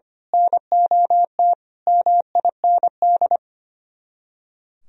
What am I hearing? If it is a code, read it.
Morse code, heard as a keyed tone, about 26 wpm, 700 Hz. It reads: NOT MIND
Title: Speed-Racing - Sets of 5 Words 25wpm